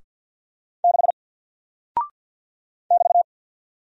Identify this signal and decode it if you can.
Morse code, heard as a keyed tone, about 48 wpm, 700 Hz. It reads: 6 <BT>
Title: Single Character (Letter:Number:Punctuation:Prosign) Alt 50wpm